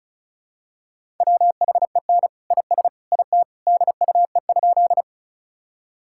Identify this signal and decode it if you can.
Morse code, heard as a keyed tone, about 36 wpm, 700 Hz. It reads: WHEN IS IT DUE?
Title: Sentences from Top 1000 Words - Review Part 6 - 35wpm